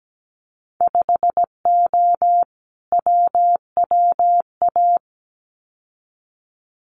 Morse code, heard as a keyed tone, about 17 wpm, 700 Hz. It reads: 5O WWA
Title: Winter Field Day Exchanges Encore 17wpm